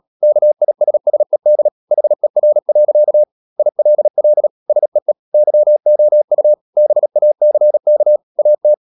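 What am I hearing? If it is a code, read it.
Morse code, heard as a keyed tone, about 37 wpm, 600 Hz. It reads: KISSED HER. ILL SEE YOU BACK AT
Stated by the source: Morse code practice at 35–39 wpm sending English text of Ken Follett's novel Never, from CW med Gnister.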